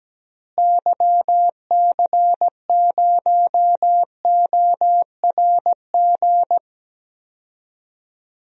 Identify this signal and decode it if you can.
Morse code, heard as a keyed tone, about 17 wpm, 700 Hz. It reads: YC0ORG
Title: DXCC-100 Call Signs - 6 Characters - Repeated 2x - 17wpm